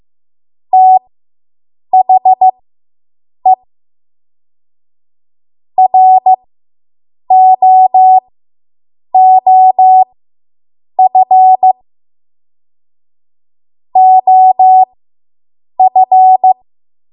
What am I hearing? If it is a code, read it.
Morse code, heard as a keyed tone, about 15 wpm, spaced as for 7 wpm, 750 Hz. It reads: THE ROOF OF